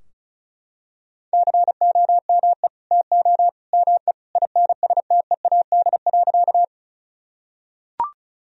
Morse code, heard as a keyed tone, about 36 wpm, 700 Hz. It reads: COME TO ME INSTEAD.
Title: Sentences from Top 1000 Words - Review Part 4 - 35wpm